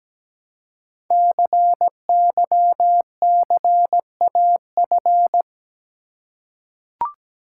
Morse code, heard as a keyed tone, about 17 wpm, 700 Hz. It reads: CYCAF
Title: Code Groups - 5 Letters - 17wpm